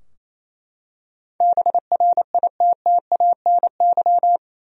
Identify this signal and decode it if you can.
Morse code, heard as a keyed tone, about 28 wpm, 700 Hz. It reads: BRITTANY
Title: USA Common Names - 28wpm